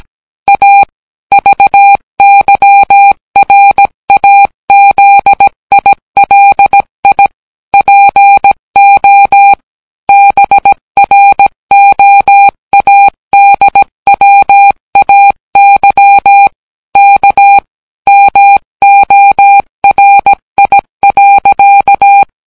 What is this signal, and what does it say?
Morse code, heard as a keyed tone, about 17 wpm, 800 Hz. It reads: A VYRAZILI PO BROADWAY K MORI.